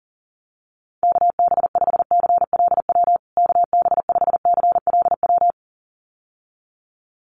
Morse code, heard as a keyed tone, about 40 wpm, 700 Hz. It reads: KB5CLW KB5CLW
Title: USA Call Signs Repeated 2x 40wpm